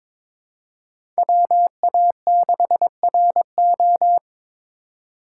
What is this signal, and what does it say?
Morse code, heard as a keyed tone, about 23 wpm, 700 Hz. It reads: WA6RO
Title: USA Call Signs - 3x Character Spacing - 22wpm